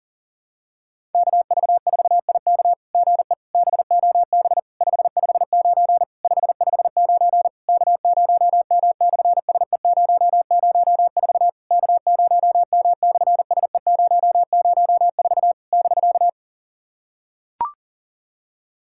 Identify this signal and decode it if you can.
Morse code, heard as a keyed tone, about 37 wpm, 700 Hz. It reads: KV4IK GE BOB 559 559 K0M/SE004 K0M/SE004 <BK>
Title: SOTA QSO Practice 40wpm